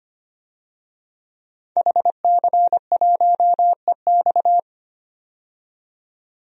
Morse code, heard as a keyed tone, about 25 wpm, 700 Hz. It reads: HC1EX
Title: DXCC-100 Call Signs - 5 Characters - 25wpm